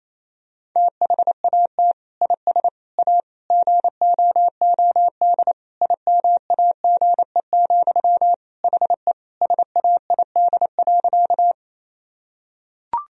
Morse code, heard as a keyed tone, about 28 wpm, 700 Hz. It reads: THAT IS A GOOD IMAGE, HE SAID.
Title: Sentences from Top 1000 Words - Review Part 4 - 28wpm